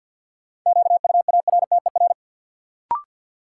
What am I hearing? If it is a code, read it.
Morse code, heard as a keyed tone, about 53 wpm, 700 Hz. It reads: QUARTER
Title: Sentences from Top 1300 Words - New Word Drill Alt - 50wpm